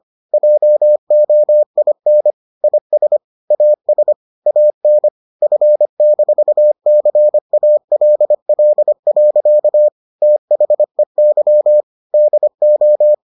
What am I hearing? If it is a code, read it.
Morse code, heard as a keyed tone, about 25 wpm, 600 Hz. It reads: JOIN IS AS AN F-CALL. THEY DO